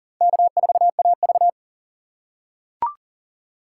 Morse code, heard as a keyed tone, about 40 wpm, 700 Hz. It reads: K4AV
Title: USA Call Signs - 6x Character Spacing - 40wpm